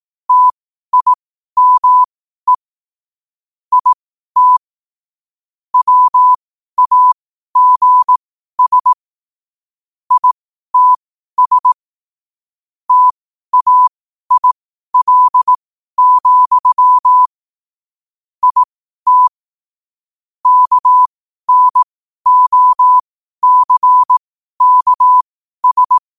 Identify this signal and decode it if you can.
Morse code, heard as a keyed tone, about 18 wpm, spaced as for 12 wpm, 1000 Hz. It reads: TIME IT WAGS ITS TAIL, IT KNOCKS
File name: qotdfiles / 12wpm